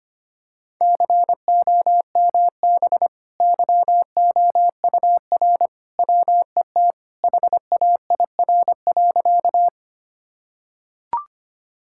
Morse code, heard as a keyed tone, about 25 wpm, 700 Hz. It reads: COMB YOUR WET HAIR.
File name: Easy Sentences with Hints - 25wpm